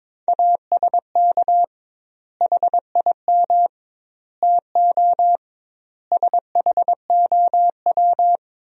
Morse code, heard as a keyed tone, about 22 wpm, 700 Hz. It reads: ASK HIM TO SHOW
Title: 2x Word-Spacing - Sentences from Top 200 Words - 22wpm